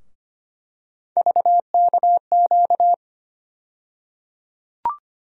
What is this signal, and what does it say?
Morse code, heard as a keyed tone, about 25 wpm, 700 Hz. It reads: VKQ
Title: USA Call Sign Postfixes - 3 Letters 25wpm